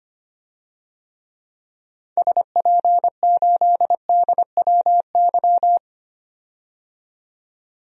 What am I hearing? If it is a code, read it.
Morse code, heard as a keyed tone, about 25 wpm, 700 Hz. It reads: SP8DWY
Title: DXCC-100 Call Signs - 6 Characters - 25wpm